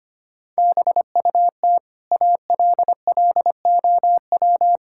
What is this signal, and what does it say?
Morse code, heard as a keyed tone, about 25 wpm, 700 Hz. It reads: BUT ALLOW
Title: Sentences from Top 1000 Words - Review Part 5 - 25wpm